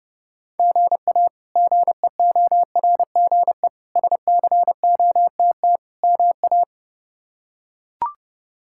Morse code, heard as a keyed tone, about 29 wpm, 700 Hz. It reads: GA GEORGE SCOTT MA
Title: K1USN SST QSO Practice - 30wpm